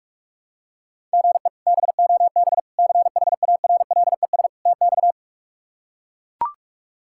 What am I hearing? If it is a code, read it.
Morse code, heard as a keyed tone, about 42 wpm, 700 Hz. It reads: GE BOB CHARLES TX
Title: K1USN SST QSO Practice - 45wpm